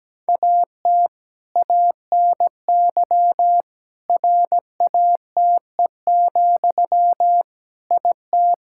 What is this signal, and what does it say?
Morse code, heard as a keyed tone, about 17 wpm, 700 Hz. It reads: AT ANY RATE, IT
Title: Sentences from Top 1600 Words 17wpm